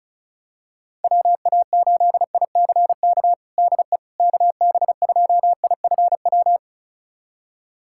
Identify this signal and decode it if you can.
Morse code, heard as a keyed tone, about 36 wpm, 700 Hz. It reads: WA8ICK DE KB2IFW
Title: USA Callsign DE Callsign 35wpm